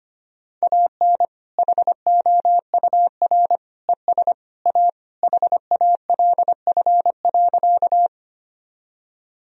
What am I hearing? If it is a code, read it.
Morse code, heard as a keyed tone, about 25 wpm, 700 Hz. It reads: AN HOUR ES A HALF.
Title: Sentences from Top 400 Words and Abbreviations 25wpm